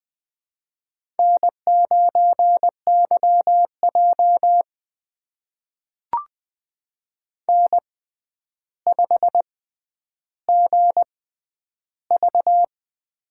Morse code, heard as a keyed tone, about 20 wpm, spaced as for 10 wpm, 700 Hz. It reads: N9YJ N5GV
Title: USA Call Signs - 4 Characters - 6x Character Spacing - 20wpm